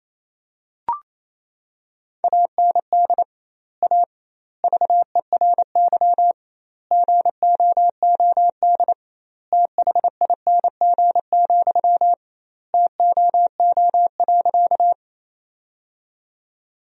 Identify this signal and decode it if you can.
Morse code, heard as a keyed tone, about 28 wpm, 700 Hz. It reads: AND A VERY GOOD THING, TOO.
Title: 2x Word-Spacing - Sentences from Top 200 Words - 28wpm